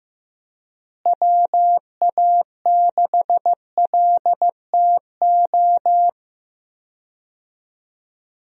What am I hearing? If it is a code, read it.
Morse code, heard as a keyed tone, about 15 wpm, 700 Hz. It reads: WA6LTO
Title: USA Call Signs - 6 Characters - Repeated 2x - 15wpm